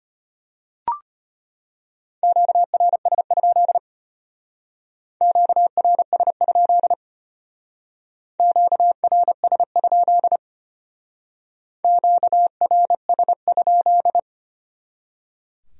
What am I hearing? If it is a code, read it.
Morse code, heard as a keyed tone, about 32 wpm, 700 Hz. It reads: QRS? QRS? QRS? QRS?